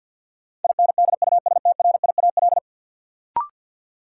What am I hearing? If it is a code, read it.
Morse code, heard as a keyed tone, about 46 wpm, 700 Hz. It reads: INDUSTRIAL
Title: Sets of 5 Words - New Word Drill - 50wpm